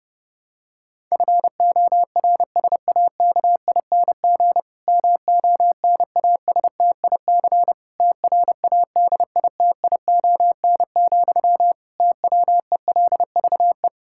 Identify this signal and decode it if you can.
Morse code, heard as a keyed tone, about 30 wpm, 700 Hz. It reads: FORSAKING MONASTIC TRADITION, TWELVE
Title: Pangram Challenge v2 - 30wpm